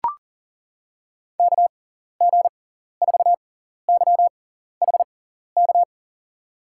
Morse code, heard as a keyed tone, about 40 wpm, spaced as for 14 wpm, 700 Hz. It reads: KG4YHK
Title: USA Call Signs - 6 Characters - 6x Character Spacing - 40wpm